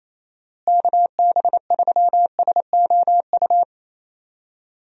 Morse code, heard as a keyed tone, about 28 wpm, 700 Hz. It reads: KB3SOU
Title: USA Call Signs - 6 Characters - Repeated 3x - 28wpm